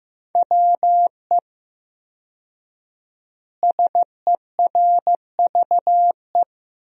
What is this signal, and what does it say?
Morse code, heard as a keyed tone, about 15 wpm, 700 Hz. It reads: WE SERVE
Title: 4x Word-Spacing - Sentences from Top 400 Words - 15wpm